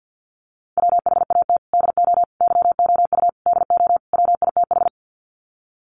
Morse code, heard as a keyed tone, about 50 wpm, 700 Hz. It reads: WHAT DO YOU DO WITH
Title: Sets of 5 Words Encore2 50wpm